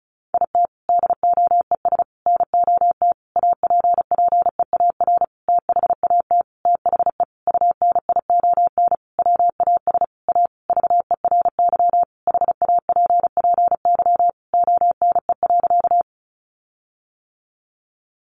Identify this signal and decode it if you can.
Morse code, heard as a keyed tone, about 34 wpm, 700 Hz. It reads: IT DOES NOT APPEAR THAT THE UNION WAS A VERY HAPPY ONE.
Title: Sentences from Top 1400 Words 35wpm